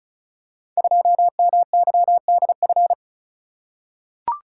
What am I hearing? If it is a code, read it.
Morse code, heard as a keyed tone, about 36 wpm, 700 Hz. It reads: 2MYDF
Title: Code Groups - 5 Letter:Numbers - 35wpm